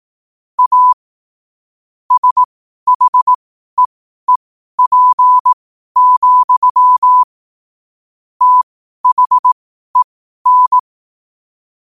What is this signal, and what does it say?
Morse code, heard as a keyed tone, about 18 wpm, spaced as for 12 wpm, 1000 Hz. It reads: A SHEEP, THEN
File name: qotdfiles / 12wpm